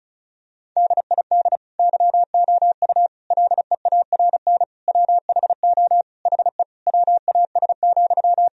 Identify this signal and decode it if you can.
Morse code, heard as a keyed tone, about 34 wpm, 700 Hz. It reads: DID YOU LEARN WHO HE WAS,
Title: Sentences from Top 1000 Words - Review Part 1 - 35wpm